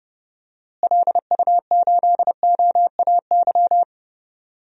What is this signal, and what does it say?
Morse code, heard as a keyed tone, about 31 wpm, 700 Hz. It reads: LU8OAY
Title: DXCC-100 Call Signs - 6 Characters - Repeated 3x - 30wpm